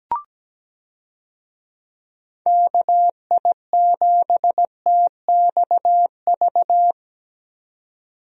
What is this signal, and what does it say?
Morse code, heard as a keyed tone, about 17 wpm, 700 Hz. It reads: KI7TXV
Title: USA Call Signs - 6 Characters - 17wpm